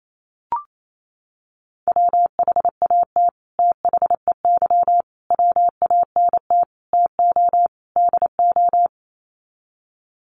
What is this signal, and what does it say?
Morse code, heard as a keyed tone, about 28 wpm, 700 Hz. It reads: WHAT THEY WANT TO DO
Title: Sets of 5 Words Encore2 28wpm